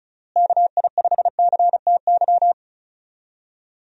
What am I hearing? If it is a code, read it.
Morse code, heard as a keyed tone, about 36 wpm, 700 Hz. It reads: KI5CTY
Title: USA Call Signs - 6 Characters - Repeated 3x - 35wpm